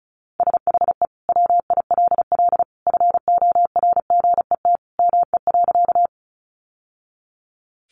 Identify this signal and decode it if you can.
Morse code, heard as a keyed tone, about 34 wpm, 700 Hz. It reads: SHE WILL FORGET ME.
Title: Sentences from Top 1200 Words 35wpm